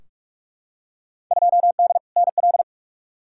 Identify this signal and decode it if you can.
Morse code, heard as a keyed tone, about 46 wpm, 700 Hz. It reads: 2D NL
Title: Field Day Exchanges Encore - 45wpm